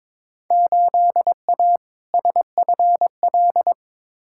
Morse code, heard as a keyed tone, about 22 wpm, 700 Hz. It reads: 8A SFL
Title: Field Day Exchanges Encore - 22wpm